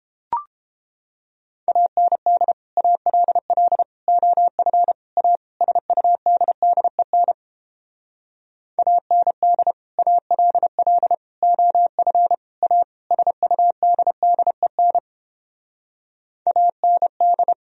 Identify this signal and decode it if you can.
Morse code, heard as a keyed tone, about 29 wpm, 700 Hz. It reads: AND ALL OF A SUDDEN AND ALL OF A SUDDEN AND